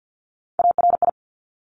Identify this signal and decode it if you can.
Morse code, heard as a keyed tone, about 50 wpm, 690 Hz. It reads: ARI
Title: Top Three Letter Combinations in Words 50wpm